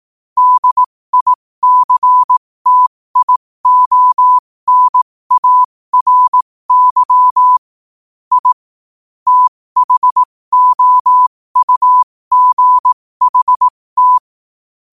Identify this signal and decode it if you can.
Morse code, heard as a keyed tone, about 18 wpm, spaced as for 15 wpm, 1000 Hz. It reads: DICTIONARY I THOUGHT
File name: qotdfiles / 15wpm